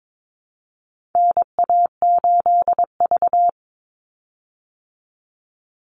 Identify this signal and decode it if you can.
Morse code, heard as a keyed tone, about 22 wpm, 700 Hz. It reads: NA8V
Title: Call Signs Repeated 3x - 22wpm